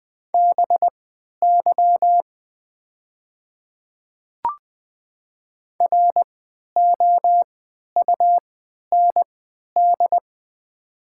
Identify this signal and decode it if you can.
Morse code, heard as a keyed tone, about 20 wpm, spaced as for 11 wpm, 700 Hz. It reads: BY ROUND
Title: 3x Character Spacing - Top 500 Words - 20wpm